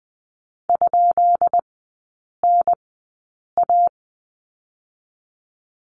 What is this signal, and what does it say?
Morse code, heard as a keyed tone, about 19 wpm, spaced as for 8 wpm, 700 Hz. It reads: ?NA
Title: 166 Lesson 27e - Number 7 - Three Characters 20wpm